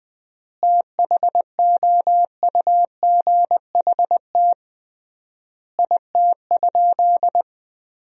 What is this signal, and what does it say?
Morse code, heard as a keyed tone, about 21 wpm, 700 Hz. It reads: THOUGHT IT?
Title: 3x Word-Spacing - Sentences from Top 300 Words - 20wpm